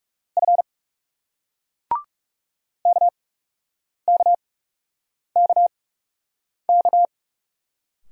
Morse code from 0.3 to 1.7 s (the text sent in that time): F